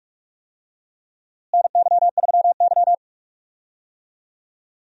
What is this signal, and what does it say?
Morse code, heard as a keyed tone, about 48 wpm, 700 Hz. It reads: NY3Y
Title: USA Call Signs - 4 Characters - Repeated 3x - 45wpm